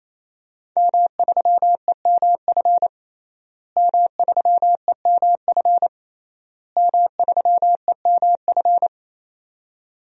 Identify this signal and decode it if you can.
Morse code, heard as a keyed tone, about 28 wpm, 700 Hz. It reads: M3EMF M3EMF M3EMF
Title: DXCC-100 Call Signs - 5 Characters - Repeated 3x - 28wpm